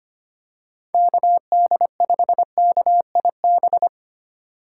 Morse code, heard as a keyed tone, about 25 wpm, 700 Hz. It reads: KD5KIB
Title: USA Call Signs - 6 Characters - 4x Character Spacing - 25wpm